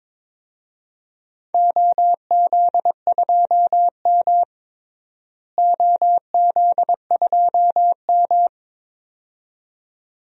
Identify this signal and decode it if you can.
Morse code, heard as a keyed tone, about 22 wpm, 700 Hz. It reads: OZ2M OZ2M
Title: DXCC-100 Call Signs - 4 Characters - Repeated 2x - 22wpm